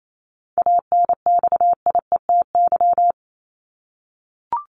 Morse code, heard as a keyed tone, about 28 wpm, 700 Hz. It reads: ANXIETY
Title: Top 1501-1750 Words 28wpm